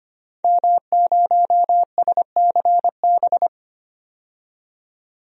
Morse code, heard as a keyed tone, about 25 wpm, 700 Hz. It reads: M0SCB
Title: DXCC-100 Call Signs - 5 Characters - 25wpm